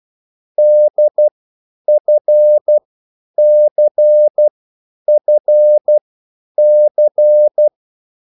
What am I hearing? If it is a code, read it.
Morse code, heard as a keyed tone, about 12 wpm, spaced as for 9 wpm, 600 Hz. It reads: DFCFC